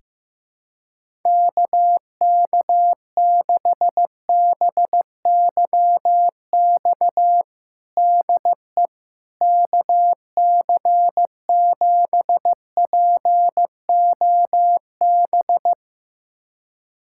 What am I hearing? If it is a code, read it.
Morse code, heard as a keyed tone, about 15 wpm, 700 Hz. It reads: KK6BYX DE KC7POB
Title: USA Callsign DE Callsign - 6 Characters - 15wpm